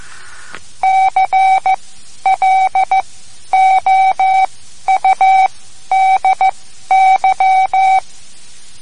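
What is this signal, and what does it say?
Morse code, heard as a keyed tone, about 15 wpm, spaced as for 11 wpm, 760 Hz, with radio noise. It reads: CLOUDY
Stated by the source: Morse code lesson, third lesson in a series for learning the Morse alphabet.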